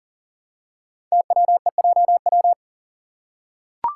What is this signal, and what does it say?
Morse code, heard as a keyed tone, about 39 wpm, 700 Hz. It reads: TWEJW
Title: Code Groups - 5 Letters - 40wpm